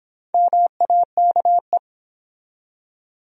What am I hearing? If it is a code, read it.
Morse code, heard as a keyed tone, about 23 wpm, 700 Hz. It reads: MAKE